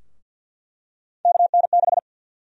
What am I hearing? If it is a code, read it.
Morse code, heard as a keyed tone, about 46 wpm, 700 Hz. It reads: KN6
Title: USA Call Sign Prefixes - Repeated 2x - 50wpm